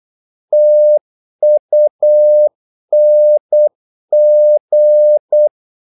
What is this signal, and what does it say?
Morse code, heard as a keyed tone, about 8 wpm, 600 Hz. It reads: TUNG
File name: Korte ord DK 8wpm